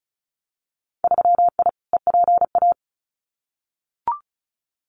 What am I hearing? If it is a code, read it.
Morse code, heard as a keyed tone, about 36 wpm, 700 Hz. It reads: 3I EPA
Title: Winter Field Day Exchanges Encore 35wpm